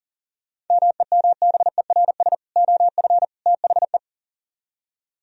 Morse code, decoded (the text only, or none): MEMBERS OF THE